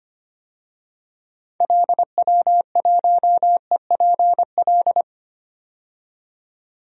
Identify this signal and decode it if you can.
Morse code, heard as a keyed tone, about 25 wpm, 700 Hz. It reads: LW1EPL
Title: DXCC-100 Call Signs - 6 Characters - 25wpm